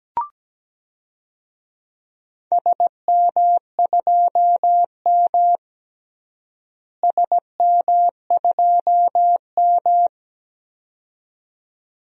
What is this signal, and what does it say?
Morse code, heard as a keyed tone, about 17 wpm, 700 Hz. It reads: SM2M SM2M
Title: DXCC-100 Call Signs - 4 Characters - Repeated 2x - 17wpm